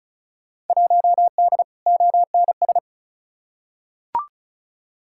Morse code, heard as a keyed tone, about 34 wpm, 700 Hz. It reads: 1D ONS
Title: Field Day Exchanges - 35wpm